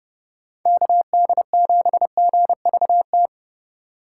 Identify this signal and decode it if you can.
Morse code, heard as a keyed tone, about 30 wpm, 700 Hz. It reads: KD7GVT
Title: USA Call Signs - 6 Characters - 6x Character Spacing - 30wpm